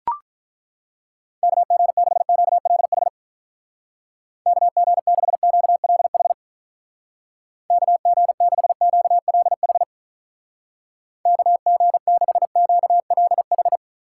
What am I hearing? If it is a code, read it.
Morse code, heard as a keyed tone, about 42 wpm, 700 Hz. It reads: KG6QLH KG6QLH KG6QLH KG6QLH
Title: Speed Racing - USA Call Signs 35wpm